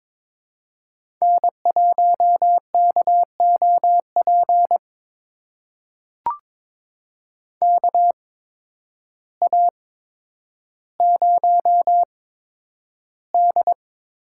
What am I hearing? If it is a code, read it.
Morse code, heard as a keyed tone, about 22 wpm, spaced as for 10 wpm, 700 Hz. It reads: N1KOP KA0D
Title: USA Call Signs - 5 Characters - 8x Character Spacing - 22wpm